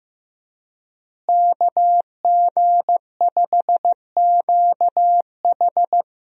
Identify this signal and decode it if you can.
Morse code, heard as a keyed tone, about 15 wpm, 700 Hz. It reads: KG5QH
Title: USA Call Signs - 5 Characters - Repeated 3x - 15wpm